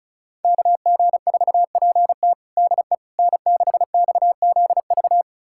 Morse code, heard as a keyed tone, about 36 wpm, 700 Hz. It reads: KG4PT DE N6XZV
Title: USA Callsign DE Callsign - 5 Characters - 35wpm